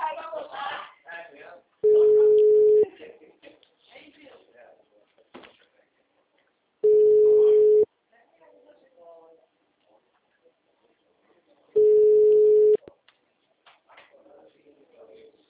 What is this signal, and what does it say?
Ringback tone (Europe, 425 Hz), with speech.